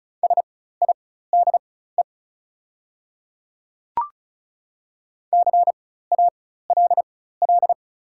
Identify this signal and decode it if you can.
Morse code, heard as a keyed tone, about 36 wpm, spaced as for 16 wpm, 690 Hz. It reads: SIDE CALL
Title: 4x Character Spacing - Top 100 Words - 35wpm